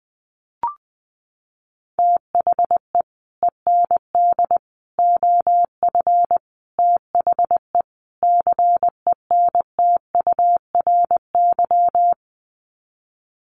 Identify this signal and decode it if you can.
Morse code, heard as a keyed tone, about 20 wpm, 700 Hz. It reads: THE END OF THE CENTURY
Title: Sets of 5 Words Encore2 20wpm